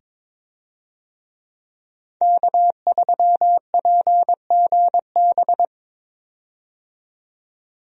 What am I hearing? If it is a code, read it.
Morse code, heard as a keyed tone, about 22 wpm, 700 Hz. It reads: K3PGB
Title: USA Call Signs - 5 Characters - 22wpm